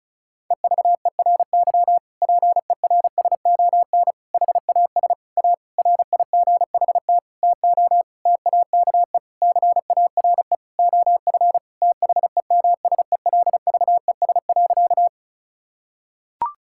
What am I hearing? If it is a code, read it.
Morse code, heard as a keyed tone, about 36 wpm, 700 Hz. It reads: EVERY PERSON HAS A RIGHT TO TAKE CARE OF THEMSELVES.
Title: Sentences from Top 1000 Words - Review Part 5 - 35wpm